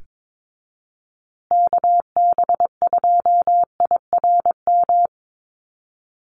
Morse code, heard as a keyed tone, about 22 wpm, 700 Hz. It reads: KB2IRM
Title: USA Call Signs Repeated 2x 22wpm